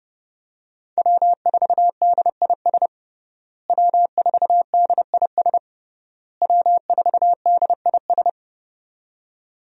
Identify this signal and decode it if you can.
Morse code, heard as a keyed tone, about 32 wpm, 700 Hz. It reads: W4DIS W4DIS W4DIS
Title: USA Call Signs - 5 Characters - Repeated 3x - 30wpm